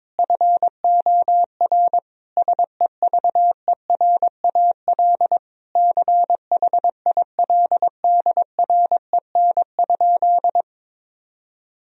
Morse code, heard as a keyed tone, about 21 wpm, 700 Hz. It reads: FOR SEVERAL CHILDREN?
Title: Sentences from Top 1000 Words - Review Part 2 - 22wpm